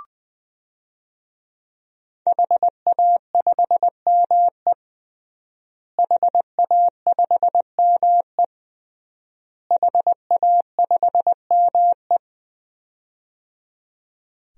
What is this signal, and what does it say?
Morse code, heard as a keyed tone, about 20 wpm, 700 Hz. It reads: HA5ME HA5ME HA5ME
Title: DXCC-100 Call Signs - 5 Characters - Repeated 3x - 20wpm